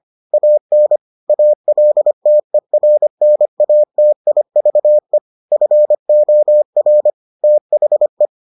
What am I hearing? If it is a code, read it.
Morse code, heard as a keyed tone, about 25 wpm, 600 Hz. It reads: AN ALTERNATIVE FOR THE